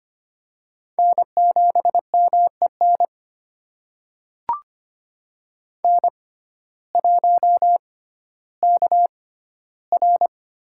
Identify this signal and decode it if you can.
Morse code, heard as a keyed tone, about 25 wpm, spaced as for 14 wpm, 700 Hz. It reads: N7MEN N1KR
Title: USA Call Signs - 5 Characters - 6x Character Spacing - 25wpm